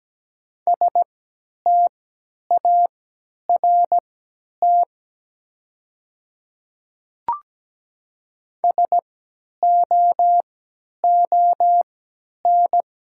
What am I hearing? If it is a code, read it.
Morse code, heard as a keyed tone, about 17 wpm, spaced as for 10 wpm, 700 Hz. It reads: START SOON
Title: 3x Character Spacing - Top 500 Words - 17wpm